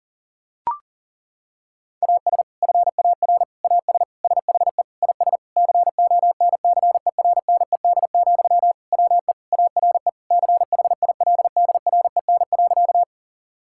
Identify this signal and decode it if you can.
Morse code, heard as a keyed tone, about 37 wpm, 700 Hz. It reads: AS FAR AS SHE IS CONCERNED, WE ARE CHILDREN.
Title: Sentences from Top 1000 Words - Review Part 6 - 40wpm